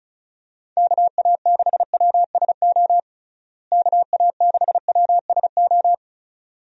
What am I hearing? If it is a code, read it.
Morse code, heard as a keyed tone, about 34 wpm, 700 Hz. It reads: KA6WSO KA6WSO
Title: USA Call Signs - 6 Characters - Repeated 2x - 35wpm